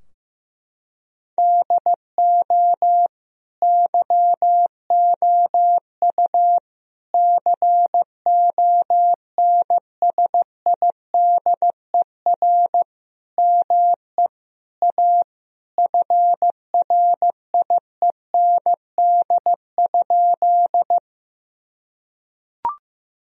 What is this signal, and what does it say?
Morse code, heard as a keyed tone, about 15 wpm, 700 Hz. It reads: DO YOU CONSIDER ME A FRIEND?
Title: Sentences from Top 1000 Words - Review Part 4 - 15wpm